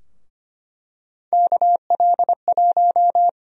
Morse code, heard as a keyed tone, about 25 wpm, 700 Hz. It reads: KL1
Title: USA Call Sign Prefixes - Repeated 2x - 25wpm